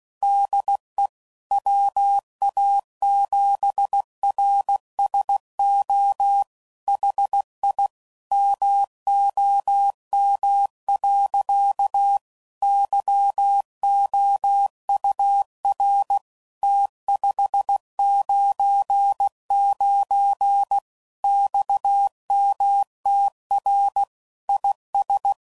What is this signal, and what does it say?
Morse code, heard as a keyed tone, about 16 wpm, 800 Hz. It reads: DE WA7RSO HI MOM. YOUR T599 XMTR IS
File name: CW Code Groups @15WPM / QSOs / WA7QOX